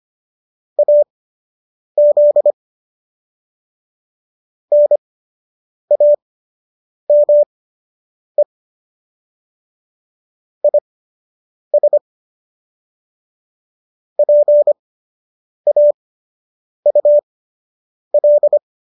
Morse code, heard as a keyed tone, about 25 wpm, spaced as for 8 wpm, 590 Hz. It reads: AZ NAME IS PAUL
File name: qso006 08